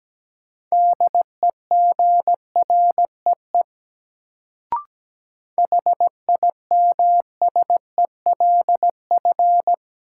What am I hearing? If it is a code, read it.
Morse code, heard as a keyed tone, about 17 wpm, 700 Hz. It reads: DEGREE HIMSELF
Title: Sentences from Top 1500 Words - New Word Drill - 17wpm